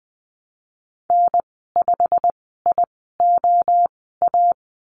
Morse code, heard as a keyed tone, about 20 wpm, spaced as for 15 wpm, 700 Hz. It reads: N5IOA
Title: 088 Lesson 15g - Number 5 - Call Signs 20wpm